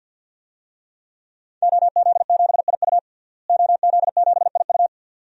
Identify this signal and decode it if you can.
Morse code, heard as a keyed tone, about 50 wpm, 700 Hz. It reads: OZ7IU OZ7IU
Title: DXCC-100 Call Signs - 5 Characters - Repeated 2x - 50wpm